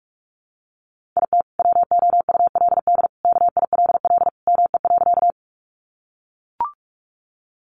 Morse code, heard as a keyed tone, about 44 wpm, 700 Hz. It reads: IT WOULD KILL ME.
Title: Sentences from Top 1400 Words 45wpm